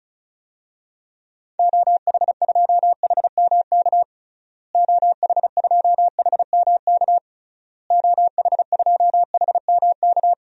Morse code, heard as a keyed tone, about 35 wpm, 700 Hz. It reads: OH2HMK OH2HMK OH2HMK
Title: DXCC-100 Call Signs - 6 Characters - Repeated 3x - 35wpm